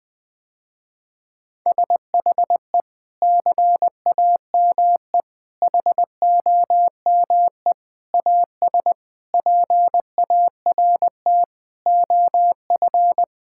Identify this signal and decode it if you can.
Morse code, heard as a keyed tone, about 20 wpm, 700 Hz. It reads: SHE CAME HOME AS PART OF
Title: Sentences with 4-Word Phrases - Spoken Beginning - 20wpm